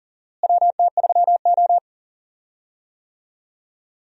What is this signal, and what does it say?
Morse code, heard as a keyed tone, about 37 wpm, 700 Hz. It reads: WT3O
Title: USA Call Signs - 4 Characters - Repeated 2x - 40wpm